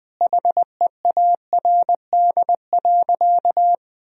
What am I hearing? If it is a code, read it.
Morse code, heard as a keyed tone, about 21 wpm, 700 Hz. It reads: HEARD.
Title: Sentences from Top 1000 Words - Review Part 1 - 20wpm